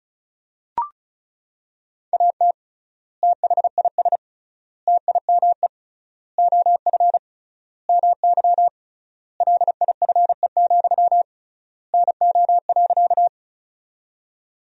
Morse code, heard as a keyed tone, about 34 wpm, 700 Hz. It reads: AT THIS TIME OF MY LIFE, NO.
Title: 3x Word-Spacing - Sentences from Top 300 Words - 35wpm